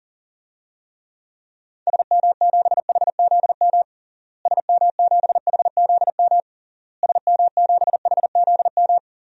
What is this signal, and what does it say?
Morse code, heard as a keyed tone, about 40 wpm, 700 Hz. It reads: SM7HZM SM7HZM SM7HZM
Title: DXCC-100 Call Signs - 6 Characters - Repeated 3x - 40wpm